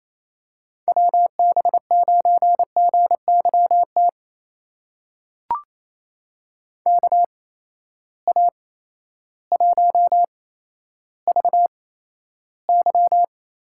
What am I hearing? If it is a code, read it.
Morse code, heard as a keyed tone, about 28 wpm, spaced as for 13 wpm, 700 Hz. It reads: WB9GYT KA1VY
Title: USA Call Signs - 6 Characters - 8x Character Spacing - 28wpm